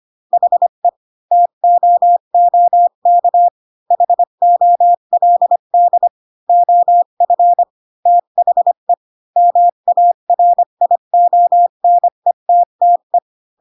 Morse code, heard as a keyed tone, about 25 wpm, 700 Hz. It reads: HE TOOK HOLD OF THE MARIONETTE
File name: Pinocchio / Pinocchio ch3 20 25 ik1hge